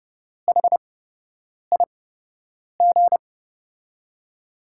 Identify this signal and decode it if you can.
Morse code, heard as a keyed tone, about 30 wpm, spaced as for 8 wpm, 700 Hz. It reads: HIG
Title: USA Call Signs - 6 Characters - 8x Character Spacing - 30wpm